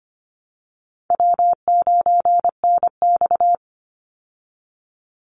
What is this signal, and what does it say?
Morse code, heard as a keyed tone, about 25 wpm, 700 Hz. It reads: W9NX
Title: Easy Call Signs Repeated 2x - 25wpm